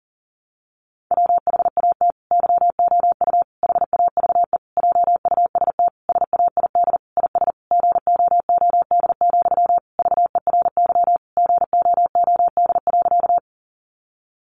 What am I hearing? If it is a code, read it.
Morse code, heard as a keyed tone, about 40 wpm, 700 Hz. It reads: WHAT YOU HAVE JUST SAID IS GOOD, VERY GOOD.
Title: Sentences from Top 200 Words 40wpm